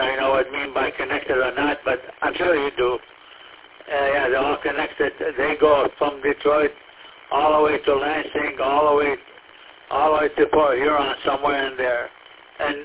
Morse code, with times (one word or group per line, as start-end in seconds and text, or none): none